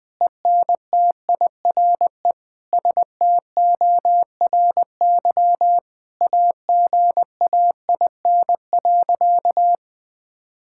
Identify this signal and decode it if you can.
Morse code, heard as a keyed tone, about 21 wpm, 700 Hz. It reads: ENTIRE STORY AGAIN.
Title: Sentences from Top 1000 Words - Review Part 5 - 20wpm